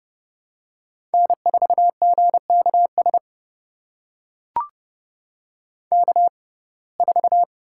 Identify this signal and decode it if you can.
Morse code, heard as a keyed tone, about 32 wpm, spaced as for 22 wpm, 700 Hz. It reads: N4GKS K4
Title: USA Call Signs - 6x Character Spacing - 30wpm